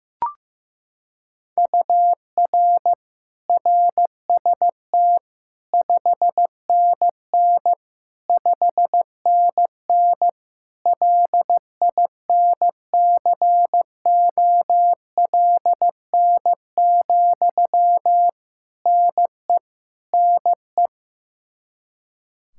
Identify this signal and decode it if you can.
Morse code, heard as a keyed tone, about 15 wpm, 700 Hz. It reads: UR RST 5NN 5NN LINCOLN, NE NE
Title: Practice QSO - Exchange 1 - Beginning - 15wpm